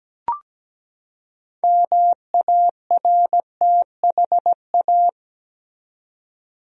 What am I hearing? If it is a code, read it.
Morse code, heard as a keyed tone, about 17 wpm, 700 Hz. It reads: MARTHA
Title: USA Common Names - 17wpm